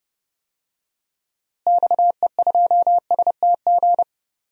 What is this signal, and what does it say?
Morse code, heard as a keyed tone, about 30 wpm, 700 Hz. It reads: XE2STG
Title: DXCC-100 Call Signs - 6 Characters - Repeated 2x - 30wpm